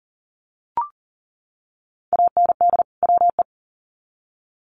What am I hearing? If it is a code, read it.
Morse code, heard as a keyed tone, about 40 wpm, 700 Hz. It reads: AND WE
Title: Sets of 2 Words 40wpm